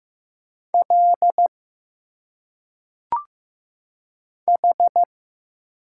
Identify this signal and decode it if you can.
Morse code, heard as a keyed tone, about 15 wpm, 700 Hz. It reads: L H
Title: Single Letters - Phonetic Pronunciation 15wpm